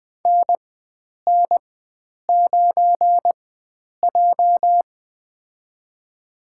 Morse code, heard as a keyed tone, about 20 wpm, spaced as for 9 wpm, 700 Hz. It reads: NN9J
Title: USA Call Signs - 4 Characters - 4x Character Spacing - 20wpm